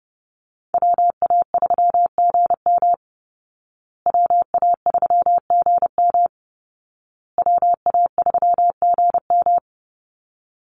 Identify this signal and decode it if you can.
Morse code, heard as a keyed tone, about 30 wpm, 700 Hz. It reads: WA3GM WA3GM WA3GM
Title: Call Signs Repeated 2x - 30wpm